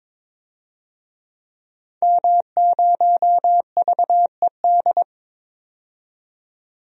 Morse code, heard as a keyed tone, about 22 wpm, 700 Hz. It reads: M0VED
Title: DXCC-100 Call Signs - 5 Characters - 22wpm